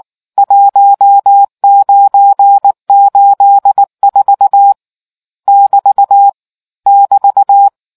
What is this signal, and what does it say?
Morse code, heard as a keyed tone, about 19 wpm, 800 Hz. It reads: 1984 <BT> <BT>